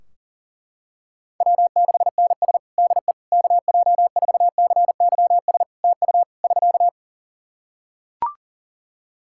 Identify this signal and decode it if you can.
Morse code, heard as a keyed tone, about 40 wpm, 700 Hz. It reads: W6NS DE KJ4CYS TU <SK>
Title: Practice QSO - Exchange 4 - Ending - 40wpm